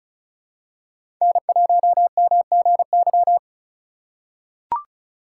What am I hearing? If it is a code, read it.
Morse code, heard as a keyed tone, about 34 wpm, 700 Hz. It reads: N1MGY
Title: USA Call Signs - 5 Characters - 3x Character Spacing - 35wpm